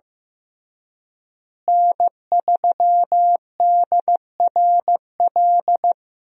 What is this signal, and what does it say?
Morse code, heard as a keyed tone, about 15 wpm, 700 Hz. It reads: N3DRL
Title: USA Call Signs - 5 Characters - Repeated 2x - 15wpm